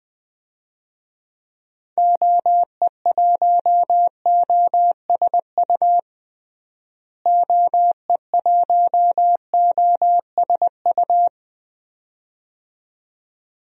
Morse code, heard as a keyed tone, about 21 wpm, 700 Hz. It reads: OE1OSU OE1OSU
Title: DXCC-100 Call Signs - 6 Characters - Repeated 2x - 20wpm